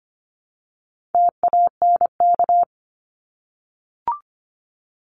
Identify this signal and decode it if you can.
Morse code, heard as a keyed tone, about 25 wpm, 700 Hz. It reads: TANK
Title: Top 1751-2000 Words 25wpm